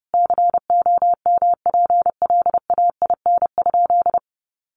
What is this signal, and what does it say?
Morse code, heard as a keyed tone, about 30 wpm, 700 Hz. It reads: COMPLAIN?
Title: Sentences from Top 1300 Words 30wpm